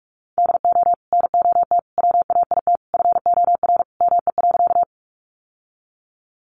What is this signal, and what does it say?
Morse code, heard as a keyed tone, about 44 wpm, 700 Hz. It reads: DO NOT WAIT FOR ME.
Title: Sentences from Top 600 Words 45wpm